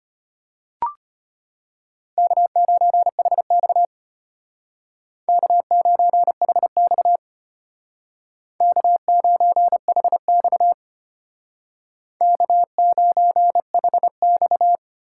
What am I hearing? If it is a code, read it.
Morse code, heard as a keyed tone, about 32 wpm, 700 Hz. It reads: K9HX K9HX K9HX K9HX